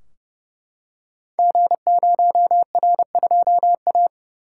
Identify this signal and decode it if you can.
Morse code, heard as a keyed tone, about 30 wpm, 700 Hz. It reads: G0R2A
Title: Code Groups - 5 Letter:Numbers - 30wpm